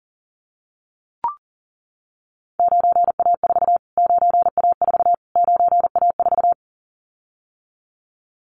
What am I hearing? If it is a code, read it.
Morse code, heard as a keyed tone, about 40 wpm, 700 Hz. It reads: 9A4 9A4 9A4
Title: Callsign Prefixes for Top 25 DXCC - Repeated 3x - 40wpm